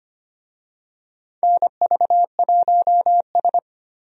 Morse code, heard as a keyed tone, about 25 wpm, 700 Hz. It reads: NV1S
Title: USA Call Signs - 4 Characters - 25wpm